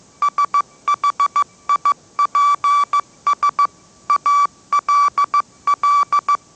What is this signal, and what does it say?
Morse code, heard as a keyed tone, about 15 wpm, 1160 Hz. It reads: SHIPS ALL
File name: mrx / mrx066 16wpm